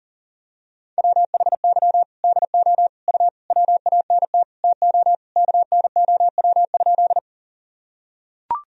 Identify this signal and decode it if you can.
Morse code, heard as a keyed tone, about 37 wpm, 700 Hz. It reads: WHY DO U WANT TO KNOW?
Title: Sentences from Top 200 Words and Abbreviations 40wpm